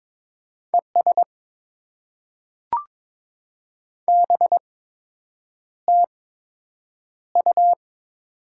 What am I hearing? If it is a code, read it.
Morse code, heard as a keyed tone, about 22 wpm, spaced as for 8 wpm, 700 Hz. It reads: ES BTU
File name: Top 20 QSO Elements - 8x Character Spacing - 22wpm